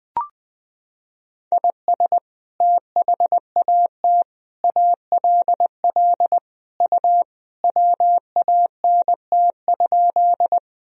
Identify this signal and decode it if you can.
Morse code, heard as a keyed tone, about 20 wpm, 700 Hz. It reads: IS THAT ALL U WANT?
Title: Sentences from Top 200 Words and Abbreviations 20wpm